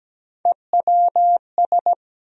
Morse code, heard as a keyed tone, about 17 wpm, 700 Hz. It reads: EWS
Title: USA Call Sign Postfixes - 3 Letters 17wpm